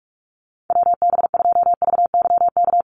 Morse code, heard as a keyed tone, about 44 wpm, 700 Hz. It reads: WB2VYK
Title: Call Signs Repeated 2x - 45wpm